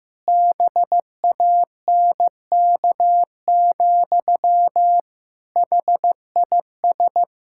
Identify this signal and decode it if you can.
Morse code, heard as a keyed tone, about 15 wpm, 700 Hz. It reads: BANK, HIS
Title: Pangram Challenge v2 - 15wpm